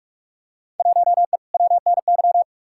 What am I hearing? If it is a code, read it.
Morse code, heard as a keyed tone, about 42 wpm, 700 Hz. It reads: 1E WNY
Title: Field Day Exchanges - 45wpm